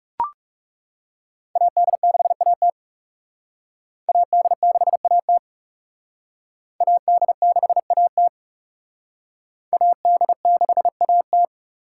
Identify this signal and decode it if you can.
Morse code, heard as a keyed tone, about 36 wpm, 700 Hz. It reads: AD6AT AD6AT AD6AT AD6AT
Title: Speed Racing - USA 2x2 Call Signs 30wpm